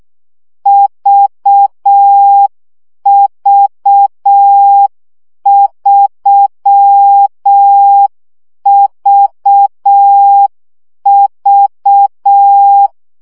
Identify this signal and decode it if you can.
Morse code, heard as a keyed tone, about 6 wpm, 800 Hz. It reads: VV3VV
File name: CW gyakorlas v 3 5 baud